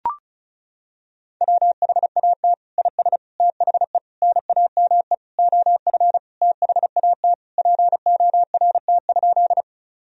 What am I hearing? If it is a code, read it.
Morse code, heard as a keyed tone, about 36 wpm, 700 Hz. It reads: WHAT IS THE NAME OF THAT PORT?
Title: Sentences from Top 1000 Words - Review Part 1 - 35wpm